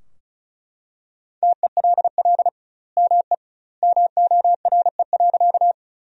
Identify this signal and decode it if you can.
Morse code, heard as a keyed tone, about 36 wpm, 700 Hz. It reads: TELL ME MORE.
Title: 2x Word-Spacing - Sentences from Top 300 Words - 35wpm